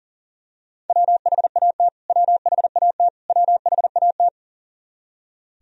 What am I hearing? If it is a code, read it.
Morse code, heard as a keyed tone, about 40 wpm, 700 Hz. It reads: WHAT WHAT WHAT
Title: Top 100 Words Repeated 40wpm